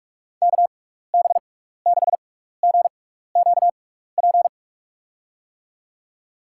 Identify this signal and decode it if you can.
Morse code, heard as a keyed tone, about 42 wpm, spaced as for 15 wpm, 700 Hz. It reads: KB6GQP
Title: USA Call Signs - 6 Characters - 6x Character Spacing - 45wpm